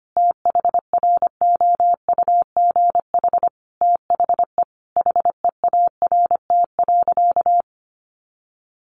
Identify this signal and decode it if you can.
Morse code, heard as a keyed tone, about 25 wpm, 700 Hz. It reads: THROUGH THE HEART.
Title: Sentences from Top 1200 Words 25wpm